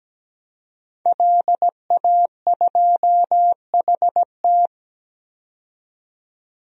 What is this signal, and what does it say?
Morse code, heard as a keyed tone, about 17 wpm, 700 Hz. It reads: LA2HT
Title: DXCC-100 Call Signs - 5 Characters - 17wpm